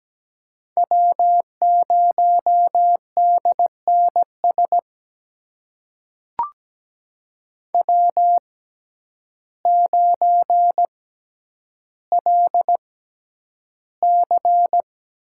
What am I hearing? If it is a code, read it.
Morse code, heard as a keyed tone, about 17 wpm, spaced as for 9 wpm, 700 Hz. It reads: W0DNS W9LC
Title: USA Call Signs - 5 Characters - 6x Character Spacing - 17wpm